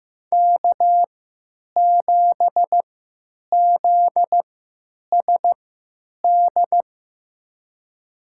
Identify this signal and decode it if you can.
Morse code, heard as a keyed tone, about 15 wpm, spaced as for 9 wpm, 700 Hz. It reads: K7ZSD
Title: USA Call Signs - 5 Characters - 3x Character Spacing - 15wpm